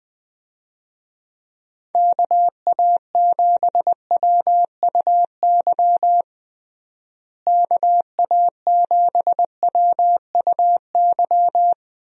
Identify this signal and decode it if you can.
Morse code, heard as a keyed tone, about 20 wpm, 700 Hz. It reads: KA7WUY KA7WUY
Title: USA Call Signs - 6 Characters - Repeated 2x - 20wpm